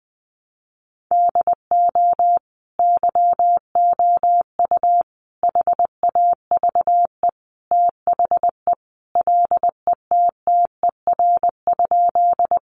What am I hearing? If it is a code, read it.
Morse code, heard as a keyed tone, about 20 wpm, 700 Hz. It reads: DO YOU HAVE THE LETTER?
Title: Sentences from Top 300 Words 20wpm